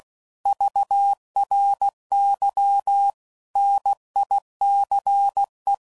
Morse code, heard as a keyed tone, about 16 wpm, 800 Hz. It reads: VRY NICE
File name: CW Code Groups @15WPM / QSOs / kc7haa